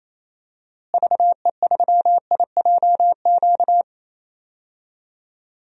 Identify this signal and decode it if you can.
Morse code, heard as a keyed tone, about 28 wpm, 700 Hz. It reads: VE3IJQ
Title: DXCC-100 Call Signs - 6 Characters - Repeated 2x - 28wpm